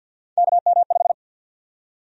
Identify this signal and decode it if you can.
Morse code, heard as a keyed tone, about 48 wpm, 700 Hz. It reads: KM5
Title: USA Call Sign Prefixes - Repeated 2x - 50wpm